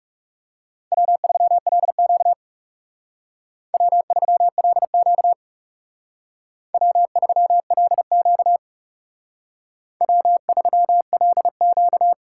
Morse code, heard as a keyed tone, about 38 wpm, 700 Hz. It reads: W3LQ W3LQ W3LQ W3LQ